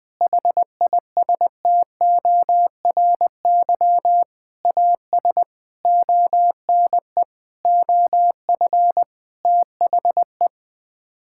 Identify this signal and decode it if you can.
Morse code, heard as a keyed tone, about 20 wpm, 700 Hz. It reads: HISTORY AS ONE OF THE
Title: Sets of 8 Words - 20wpm